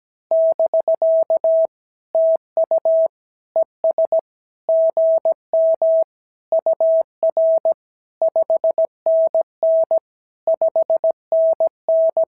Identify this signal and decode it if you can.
Morse code, heard as a keyed tone, about 17 wpm, 650 Hz. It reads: <BK> TU ES GM UR 5NN 5NN
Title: SOTA QSO Practice 17wpm